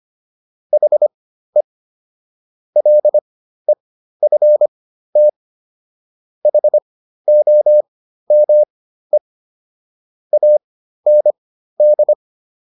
Morse code, heard as a keyed tone, about 25 wpm, spaced as for 13 wpm, 600 Hz. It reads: HE LEFT HOME AND